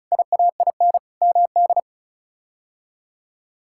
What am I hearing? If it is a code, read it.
Morse code, heard as a keyed tone, about 36 wpm, 700 Hz. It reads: IAIN MD
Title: K1USN SST QSO Practice - 35wpm